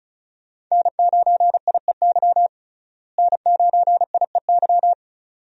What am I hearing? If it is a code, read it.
Morse code, heard as a keyed tone, about 36 wpm, 700 Hz. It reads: N9IEY N9IEY
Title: USA Call Signs - 5 Characters - Repeated 2x - 35wpm